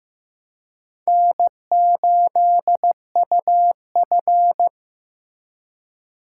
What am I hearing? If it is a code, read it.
Morse code, heard as a keyed tone, about 15 wpm, 700 Hz. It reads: N8UF
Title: USA Call Signs - 3x Character Spacing - 15wpm